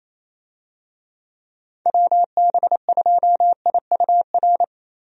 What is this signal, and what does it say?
Morse code, heard as a keyed tone, about 28 wpm, 700 Hz. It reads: WB2IUR
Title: USA Call Signs - 6 Characters - 28wpm